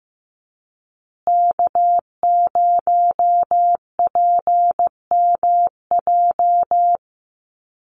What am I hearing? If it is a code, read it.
Morse code, heard as a keyed tone, about 15 wpm, 700 Hz. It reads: K0PMJ
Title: USA Call Signs Repeated 3x 15wpm